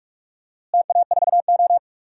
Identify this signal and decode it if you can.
Morse code, heard as a keyed tone, about 46 wpm, 700 Hz. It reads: TA4O
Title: DXCC-100 Call Signs - 4 Characters - Repeated 3x - 45wpm